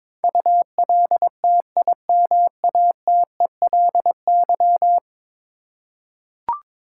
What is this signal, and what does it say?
Morse code, heard as a keyed tone, about 22 wpm, 700 Hz. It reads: ULTIMATELY
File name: practice / 22wpm / 10 Letter Words - First 3 Letters Given - 22wpm